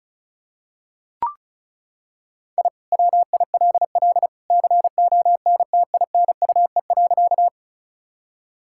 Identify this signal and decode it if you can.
Morse code, heard as a keyed tone, about 36 wpm, 700 Hz. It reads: I WILL CONTINUE.
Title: Sentences from Top 1000 Words - Review Part 6 - 35wpm